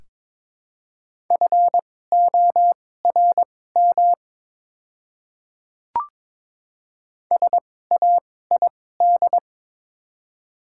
Morse code, heard as a keyed tone, about 22 wpm, spaced as for 16 wpm, 700 Hz. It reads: FORM SAID
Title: 2x Character Spacing - Top 200 Words - 22wpm